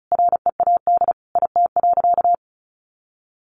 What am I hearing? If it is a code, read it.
Morse code, heard as a keyed tone, about 34 wpm, 700 Hz. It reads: READ IT.
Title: Sentences from Top 500 Words 35wpm